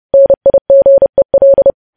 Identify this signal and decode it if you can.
Morse code, heard as a keyed tone, about 30 wpm, 550 Hz. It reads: NIGEL
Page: news headlines in morse code at 30 wpm